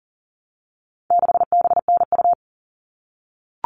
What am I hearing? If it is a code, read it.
Morse code, heard as a keyed tone, about 40 wpm, 700 Hz. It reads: 6BNU
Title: All Cut Numbers - 4-Digit 40wpm